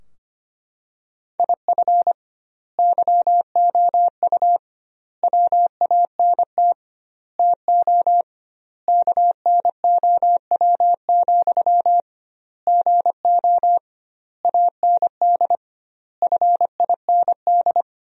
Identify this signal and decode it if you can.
Morse code, heard as a keyed tone, about 25 wpm, 700 Hz. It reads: IF YOU WANT TO KNOW, GO AND FIND
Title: 2x Word-Spacing - Sentences from Top 200 Words - 25wpm